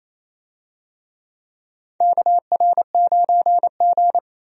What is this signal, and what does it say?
Morse code, heard as a keyed tone, about 28 wpm, 690 Hz. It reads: KR9G
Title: USA Call Signs - 4 Characters - 28wpm